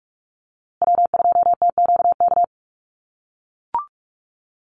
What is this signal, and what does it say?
Morse code, heard as a keyed tone, about 44 wpm, 690 Hz. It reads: W2TQK
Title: USA Call Signs Repeated 3x 45wpm